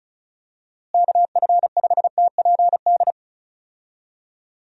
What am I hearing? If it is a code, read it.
Morse code, heard as a keyed tone, about 34 wpm, 700 Hz. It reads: KF5TPD
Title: USA Call Signs - 6 Characters - Repeated 2x - 35wpm